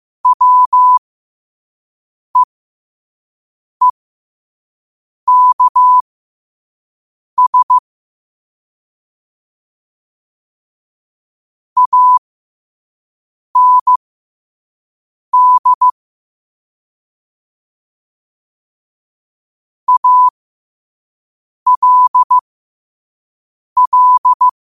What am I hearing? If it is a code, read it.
Morse code, heard as a keyed tone, about 15 wpm, spaced as for 5 wpm, 1000 Hz. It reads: WEEKS AND ALL
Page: New quotes every day in morse code at 5 Words per minute.